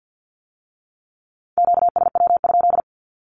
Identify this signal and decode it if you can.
Morse code, heard as a keyed tone, about 50 wpm, 700 Hz. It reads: QSW?
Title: Q-Codes with Spoken Meaning - 50wpm